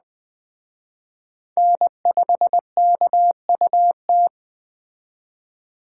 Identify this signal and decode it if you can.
Morse code, heard as a keyed tone, about 20 wpm, 700 Hz. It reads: N5KUT
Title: Speed Racing - USA 1x3 Call Signs 17wpm